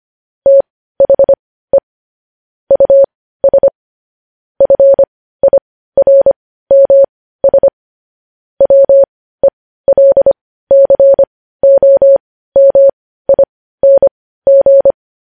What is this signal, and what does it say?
Morse code, heard as a keyed tone, about 25 wpm, spaced as for 15 wpm, 550 Hz. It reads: THE US FIRMS WELCOMING
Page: news-headlines-in-morse-code-at-15-wpm